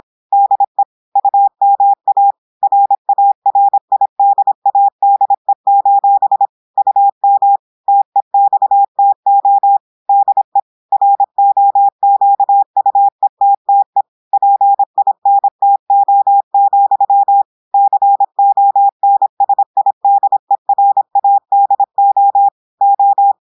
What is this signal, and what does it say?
Morse code, heard as a keyed tone, about 26 wpm, 800 Hz. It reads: DE UMA RARIDADE: UM TEXTO DE ROQUETTE PINTO, CONSIDERADO O
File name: audios / portugais / portugais texte en clair 1 vitesse 26 mots minutes